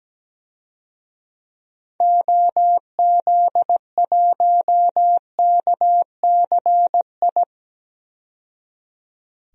Morse code, heard as a keyed tone, about 17 wpm, 700 Hz. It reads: OZ1KCI
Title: DXCC-100 Call Signs - 6 Characters - 17wpm